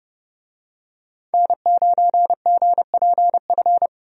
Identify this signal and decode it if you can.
Morse code, heard as a keyed tone, about 30 wpm, 700 Hz. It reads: N9GPF
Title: USA Call Signs - 5 Characters - 30wpm